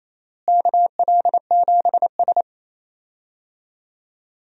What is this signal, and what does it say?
Morse code, heard as a keyed tone, about 28 wpm, 700 Hz. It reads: KL7S
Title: DXCC-100 Call Signs - 4 Characters - Repeated 3x - 28wpm